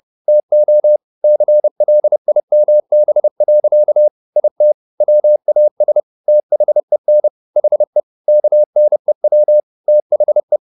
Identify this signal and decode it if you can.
Morse code, heard as a keyed tone, about 30 wpm, 600 Hz. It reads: TO CLIMB. IT WAS THEN HE KNEW THE